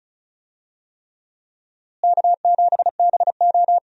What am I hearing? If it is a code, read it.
Morse code, heard as a keyed tone, about 34 wpm, 700 Hz. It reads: K7BO
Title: USA Call Signs - 4 Characters - Repeated 3x - 35wpm